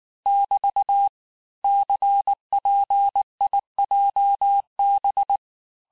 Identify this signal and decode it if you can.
Morse code, heard as a keyed tone, about 19 wpm, 800 Hz. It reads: <BT> CPIJB
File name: cours-du-lundi files / cours-1583--texte-chiffre-18-mpm